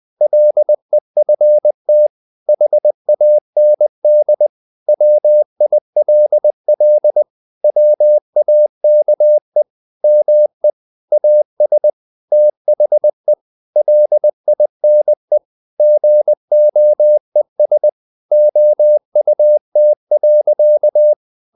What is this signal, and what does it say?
Morse code, heard as a keyed tone, about 20 wpm, 600 Hz. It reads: LEFT HAND WILL WAKE ME AS THE LINE GOES OUT.
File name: the old man and the sea part 2 20wpm